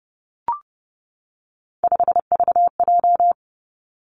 Morse code, heard as a keyed tone, about 31 wpm, 700 Hz. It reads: 5VJ
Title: Three Challenging Letter-Number Combinations 30wpm